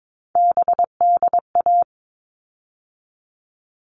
Morse code, heard as a keyed tone, about 22 wpm, 700 Hz. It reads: BDA
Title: All Cut Numbers - 3-Digit 22wpm